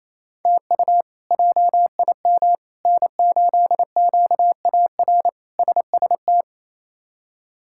Morse code, heard as a keyed tone, about 28 wpm, 700 Hz. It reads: TU JIM N8QAR SST
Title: K1USN SST QSO Practice - 28wpm